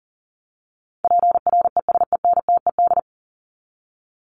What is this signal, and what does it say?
Morse code, heard as a keyed tone, about 40 wpm, 700 Hz. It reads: PRESENTED
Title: ED Suffix - Top 100 Words with Base Words 40wpm